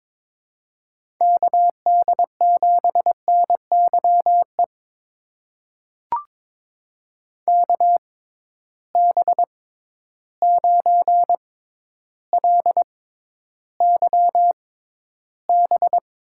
Morse code, heard as a keyed tone, about 22 wpm, spaced as for 11 wpm, 700 Hz. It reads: KD7NYE KB9LYB
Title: USA Call Signs - 6 Characters - 6x Character Spacing - 22wpm